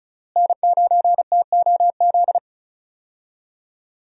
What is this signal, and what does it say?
Morse code, heard as a keyed tone, about 36 wpm, 700 Hz. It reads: N9TOZ
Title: USA Call Signs - 5 Characters - Repeated 3x - 35wpm